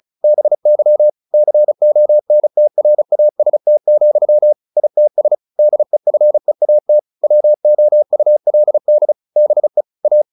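Morse code, heard as a keyed tone, about 34 wpm, 600 Hz. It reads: BY CONTRAST, ITS DEFEAT WOULD BE A
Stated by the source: Morse practice at 35 wpm from CW med Gnister, sending Ken Follett's novel Never in English.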